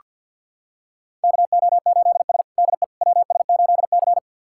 Morse code, heard as a keyed tone, about 48 wpm, 700 Hz. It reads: KO8S DE WS8C
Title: USA Callsign DE Callsign - 4 Characters - 50wpm